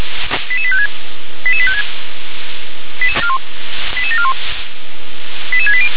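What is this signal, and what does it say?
Five-tone selective call (ZVEI), heard as quick runs of five tones, tones of about 70 ms. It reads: 88856 88856 88741 88680
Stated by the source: ZVEI five-tone selective call received by SDR.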